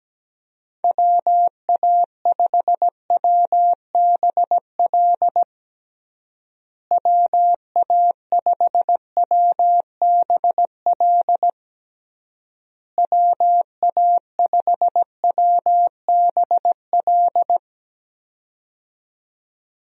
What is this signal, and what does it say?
Morse code, heard as a keyed tone, about 17 wpm, 700 Hz. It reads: WA5WBL WA5WBL WA5WBL
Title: USA Call Signs - 6 Characters - Repeated 3x - 17wpm